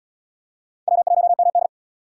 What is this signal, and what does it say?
Morse code, heard as a keyed tone, about 73 wpm, 700 Hz. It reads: W1MD